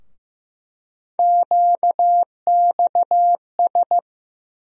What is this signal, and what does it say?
Morse code, heard as a keyed tone, about 15 wpm, 700 Hz. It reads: QXS
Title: USA Call Sign Postfixes - 3 Letters 15wpm